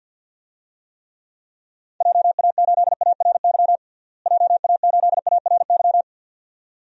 Morse code, heard as a keyed tone, about 52 wpm, 700 Hz. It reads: JA8ARY JA8ARY
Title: DXCC-100 Call Signs - 6 Characters - Repeated 2x - 50wpm